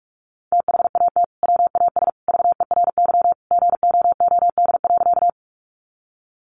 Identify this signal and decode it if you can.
Morse code, heard as a keyed tone, about 44 wpm, 700 Hz. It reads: THAT WAS VERY GOOD.
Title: Sentences from Top 200 Words 45wpm